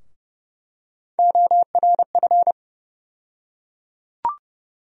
Morse code, heard as a keyed tone, about 31 wpm, 700 Hz. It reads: ORF
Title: USA Call Sign Postfixes - 3 Letters 30wpm